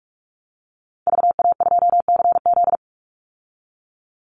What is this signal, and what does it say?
Morse code, heard as a keyed tone, about 44 wpm, 700 Hz. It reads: VA2CZ
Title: Call Signs Repeated 3x - 45wpm